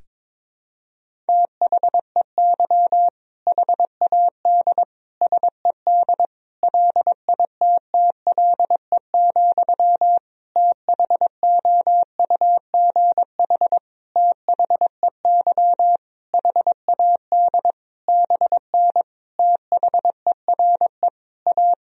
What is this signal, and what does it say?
Morse code, heard as a keyed tone, about 22 wpm, 700 Hz. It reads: THEY HAD SED LITTLE, THOUGH THEY HAD BN THERE A
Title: Sentences from Top 400 Words and Abbreviations 22wpm